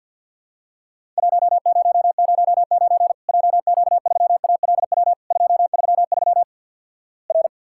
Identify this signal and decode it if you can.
Morse code, heard as a keyed tone, about 50 wpm, 700 Hz. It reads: 1009 JQ3ALW 233 R
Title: Worked All Europe DX Contest - QTC Exchange - 50wpm